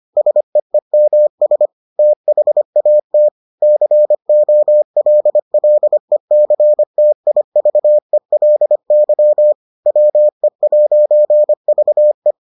Morse code, heard as a keyed tone, about 25 wpm, 600 Hz. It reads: SEEMS THAT COLLECTIVELY WE'VE